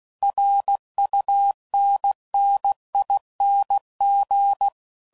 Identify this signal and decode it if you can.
Morse code, heard as a keyed tone, about 16 wpm, 800 Hz. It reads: RUNNING
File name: CW Code Groups @15WPM / QSOs / WD0FBT Spaced